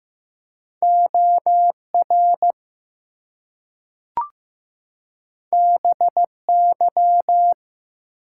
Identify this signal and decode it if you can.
Morse code, heard as a keyed tone, about 15 wpm, 700 Hz. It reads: OR BY
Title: Two Letter Words - 15wpm